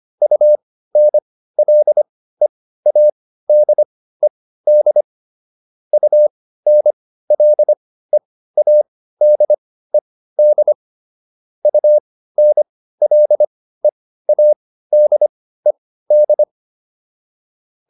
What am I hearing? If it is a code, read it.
Morse code, heard as a keyed tone, about 25 wpm, spaced as for 15 wpm, 600 Hz. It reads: UNLEADED UNLEADED UNLEADED